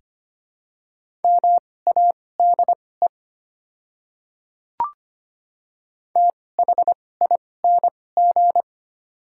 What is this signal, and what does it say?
Morse code, heard as a keyed tone, about 25 wpm, spaced as for 18 wpm, 700 Hz. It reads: MADE THING
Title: 2x Character Spacing - Top 300 Words - 25wpm